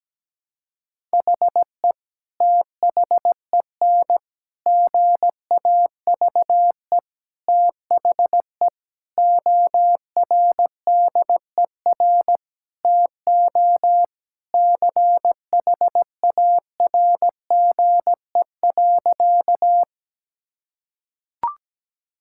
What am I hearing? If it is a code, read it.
Morse code, heard as a keyed tone, about 17 wpm, 700 Hz. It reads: HE THEN GAVE THE ORDER TO CHARGE.
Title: Sentences from Top 1000 Words - Review Part 6 - 17wpm